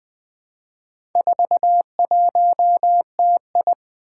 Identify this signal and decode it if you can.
Morse code, heard as a keyed tone, about 20 wpm, 700 Hz. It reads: 41TI
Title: Four Challenging Letter-Number Combinations 20wpm